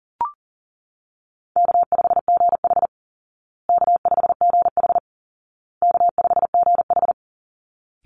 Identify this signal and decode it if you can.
Morse code, heard as a keyed tone, about 37 wpm, 700 Hz. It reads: K5GH K5GH K5GH
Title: Easy Call Signs Repeated 3x - 40wpm